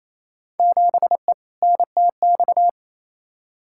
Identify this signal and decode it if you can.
Morse code, heard as a keyed tone, about 28 wpm, 700 Hz. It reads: 7E NTX
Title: Field Day Exchanges Encore - 28wpm